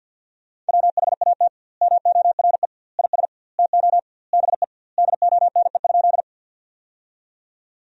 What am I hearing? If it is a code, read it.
Morse code, heard as a keyed tone, about 52 wpm, 700 Hz. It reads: WHAT MORE IS TO BE DONE?
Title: 2x Word-Spacing - Sentences from Top 500 Words - 50wpm